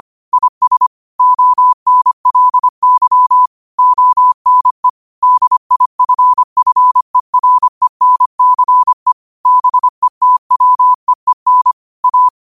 Morse code, heard as a keyed tone, about 25 wpm, 1000 Hz. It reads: IS ONLY ONE DIFFERENCE BETWEEN A